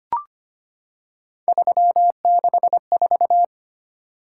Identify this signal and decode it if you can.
Morse code, heard as a keyed tone, about 25 wpm, 700 Hz. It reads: 364
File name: Three Letter-Number Combinations - 25wpm